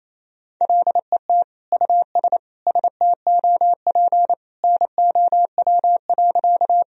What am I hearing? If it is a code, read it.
Morse code, heard as a keyed tone, about 28 wpm, 700 Hz. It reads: LET US STOP NOW.
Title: Sentences from Top 1000 Words - Review Part 1 - 28wpm